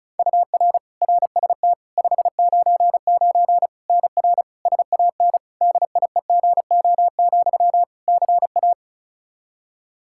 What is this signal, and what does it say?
Morse code, heard as a keyed tone, about 34 wpm, 700 Hz. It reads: UR RST 599 NR SAN DIEGO, CA
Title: Practice QSO - Exchange 1 - Beginning - 35wpm